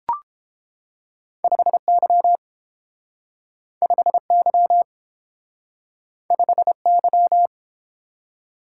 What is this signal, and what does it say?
Morse code, heard as a keyed tone, about 32 wpm, 700 Hz. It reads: 5Y 5Y 5Y